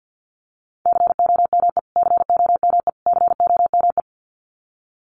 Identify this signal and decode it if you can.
Morse code, heard as a keyed tone, about 50 wpm, 700 Hz. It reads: COME COME COME
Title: Top 500 Words Repeated 50wpm